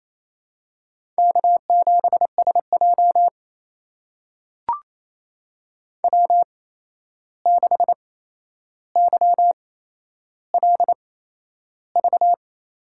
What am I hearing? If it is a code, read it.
Morse code, heard as a keyed tone, about 28 wpm, spaced as for 11 wpm, 700 Hz. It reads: K7SJ W6YLV
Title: USA Call Signs - 8x Character Spacing - 28wpm